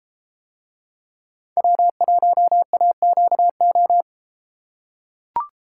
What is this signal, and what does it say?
Morse code, heard as a keyed tone, about 31 wpm, 700 Hz. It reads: W1AQO